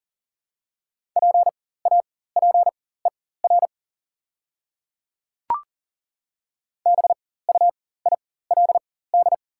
Morse code, heard as a keyed tone, about 40 wpm, spaced as for 19 wpm, 690 Hz. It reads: PAPER BUILD
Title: 4x Character Spacing - Top 500 Words - 40wpm